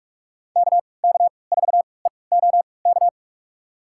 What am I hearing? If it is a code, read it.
Morse code, heard as a keyed tone, about 45 wpm, spaced as for 25 wpm, 700 Hz. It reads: KK4EOK
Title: USA Call Signs - 3x Character Spacing - 45wpm